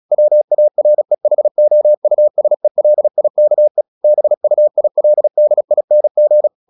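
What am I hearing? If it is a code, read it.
Morse code, heard as a keyed tone, about 36 wpm, 600 Hz. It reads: WAREHOUSELIKE BUILDING
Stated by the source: Morse practice at 36 wpm, text of Ken Follett's novel Never, from CW med Gnister.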